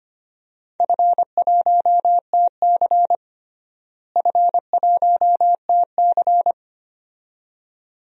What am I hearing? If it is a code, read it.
Morse code, heard as a keyed tone, about 25 wpm, 700 Hz. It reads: F1TC F1TC
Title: DXCC-100 Call Signs - 4 Characters - Repeated 2x - 25wpm